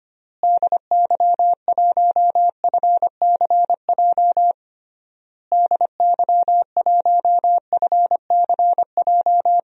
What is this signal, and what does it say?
Morse code, heard as a keyed tone, about 25 wpm, 700 Hz. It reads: DY1FCJ DY1FCJ
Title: DXCC-100 Call Signs - 6 Characters - Repeated 3x - 25wpm